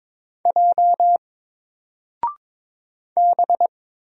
Morse code, heard as a keyed tone, about 22 wpm, 700 Hz. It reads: J B
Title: Single Character (Letter:Number:Punctuation:Prosign) 22wpm